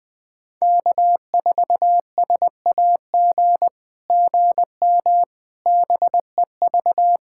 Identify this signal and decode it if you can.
Morse code, heard as a keyed tone, about 20 wpm, 700 Hz. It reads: K4SAG GM BEV
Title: SOTA QSO Practice 20wpm